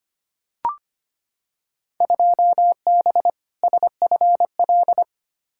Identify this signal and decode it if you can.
Morse code, heard as a keyed tone, about 25 wpm, 700 Hz. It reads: 2B SFL
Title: Field Day Exchanges Encore - 25wpm